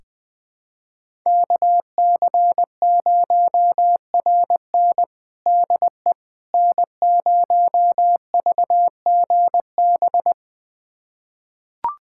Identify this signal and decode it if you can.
Morse code, heard as a keyed tone, about 20 wpm, 700 Hz. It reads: KC0RN DE N0VGB
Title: USA Callsign DE Callsign 20wpm